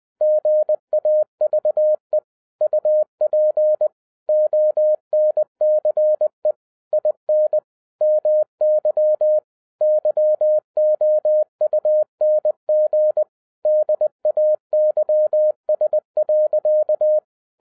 Morse code, heard as a keyed tone, about 20 wpm, 610 Hz. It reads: GAVE UP ONCE IN MY YOUNG DAYS.
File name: 2012-01-mp3 / n05-CAME BACK TO HAM RADIO AFTER 30 YEARS OF INTERVAL-20wpm